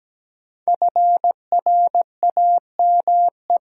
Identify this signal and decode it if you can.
Morse code, heard as a keyed tone, about 17 wpm, 700 Hz. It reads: FRAME
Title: Sentences from Top 1400 Words - New Word Drill Alt - 17wpm